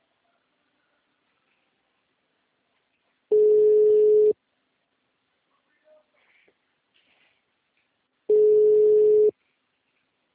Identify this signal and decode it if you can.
Ringback tone (Europe, 425 Hz).
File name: out- -2024- - - .6